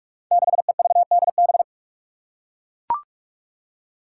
Morse code, heard as a keyed tone, about 44 wpm, 700 Hz. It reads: 6EVDB
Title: Five Challenging Letter-Number Combinations 45wpm